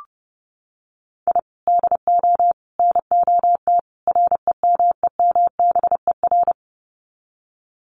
Morse code, heard as a keyed tone, about 30 wpm, 700 Hz. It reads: I DO NOT REMEMBER
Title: Sets of 4 Words 30wpm